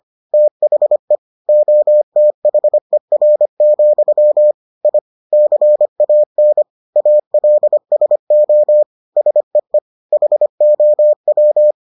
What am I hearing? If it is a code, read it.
Morse code, heard as a keyed tone, about 25 wpm, 600 Hz. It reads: THE OTHER, I CAN ALSO SEE HOW